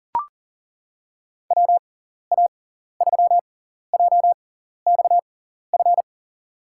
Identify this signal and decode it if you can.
Morse code, heard as a keyed tone, about 40 wpm, spaced as for 14 wpm, 700 Hz. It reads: WA3JXF
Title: USA Call Signs - 6 Characters - 6x Character Spacing - 40wpm